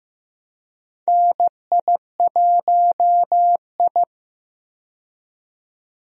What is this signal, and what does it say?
Morse code, heard as a keyed tone, about 15 wpm, 700 Hz. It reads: NI1I
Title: USA Call Signs - 4 Characters - 15wpm